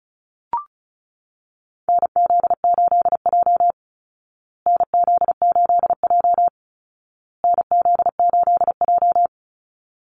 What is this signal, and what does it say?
Morse code, heard as a keyed tone, about 34 wpm, 700 Hz. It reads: NZ8J NZ8J NZ8J
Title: Call Signs Repeated 2x - 35wpm